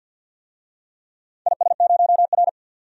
Morse code, heard as a keyed tone, about 48 wpm, 700 Hz. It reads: IS0R
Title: DXCC-100 Call Signs - 4 Characters - 50wpm